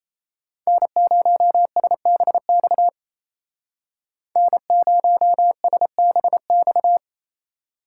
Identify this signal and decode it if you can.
Morse code, heard as a keyed tone, about 36 wpm, 700 Hz. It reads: N0SBX N0SBX